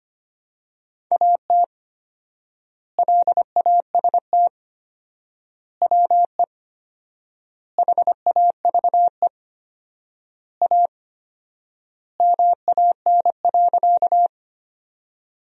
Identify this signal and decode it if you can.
Morse code, heard as a keyed tone, about 25 wpm, 700 Hz. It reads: AT LAST WE HAVE A MAN.
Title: 4x Word-Spacing - Sentences from Top 300 Words - 25wpm